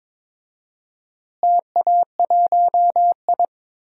Morse code, heard as a keyed tone, about 22 wpm, 700 Hz. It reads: TA1I
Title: DXCC-100 Call Signs - 4 Characters - Repeated 3x - 22wpm